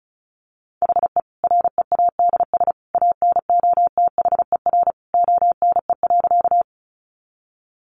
Morse code, heard as a keyed tone, about 34 wpm, 700 Hz. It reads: HE READS ANOTHER ONE.
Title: Additional Easy Phrases 35wpm